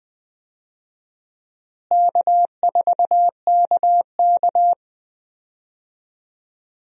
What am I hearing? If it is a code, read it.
Morse code, heard as a keyed tone, about 20 wpm, 700 Hz. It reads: K4KK
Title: USA Call Signs - 4 Characters - Repeated 3x - 20wpm